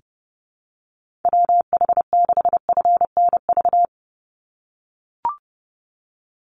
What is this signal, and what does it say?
Morse code, heard as a keyed tone, about 30 wpm, 700 Hz. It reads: WH6FNV
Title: USA Call Signs 30wpm